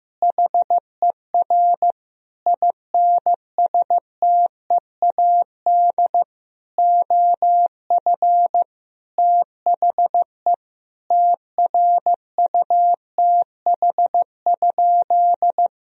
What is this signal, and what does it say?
Morse code, heard as a keyed tone, about 15 wpm, 700 Hz. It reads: HER INSTEAD OF THE TRUTH?
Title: Sentences from Top 1000 Words - Review Part 4 - 15wpm